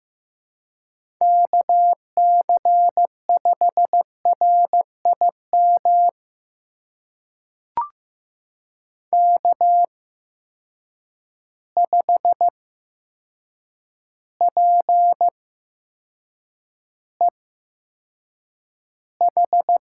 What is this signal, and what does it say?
Morse code, heard as a keyed tone, about 15 wpm, spaced as for 7 wpm, 700 Hz. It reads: KC5RIM K5PEH
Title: USA Call Signs - 8x Character Spacing - 15wpm